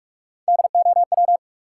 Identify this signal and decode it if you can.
Morse code, heard as a keyed tone, about 46 wpm, 700 Hz. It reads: DOW
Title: Sets of 5 Words - New Word Drill - 45wpm